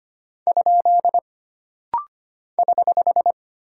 Morse code, heard as a keyed tone, about 25 wpm, 700 Hz. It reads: ? <HH>
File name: Single Punctuation-Prosign - Rapid-Fire - 25wpm